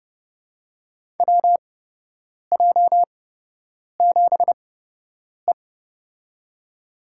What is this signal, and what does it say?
Morse code, heard as a keyed tone, about 32 wpm, spaced as for 8 wpm, 700 Hz. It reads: WJ7E
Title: USA Call Signs - 4 Characters - 8x Character Spacing - 30wpm